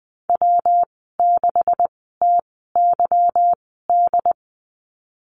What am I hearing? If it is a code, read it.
Morse code, heard as a keyed tone, about 19 wpm, spaced as for 14 wpm, 700 Hz. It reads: W6TYD
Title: 147 Lesson 24g - Letter y - Call Signs 20wpm